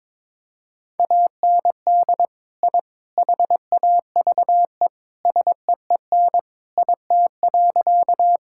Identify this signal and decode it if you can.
Morse code, heard as a keyed tone, about 21 wpm, 700 Hz. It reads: AND I HAVE SEEN IT.
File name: practice / 22wpm / Sentences with 3-Word Phrases - 22wpm